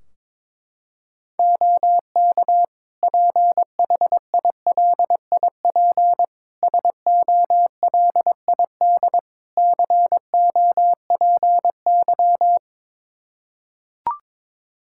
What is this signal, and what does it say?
Morse code, heard as a keyed tone, about 22 wpm, 700 Hz. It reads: OK PHILIP SOLID COPY
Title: K1USN SST QSO Practice 22wpm